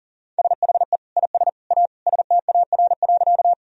Morse code, heard as a keyed tone, about 37 wpm, 700 Hz. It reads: SHE IS A STAR.
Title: Sentences from Top 1000 Words - Review Part 2 - 40wpm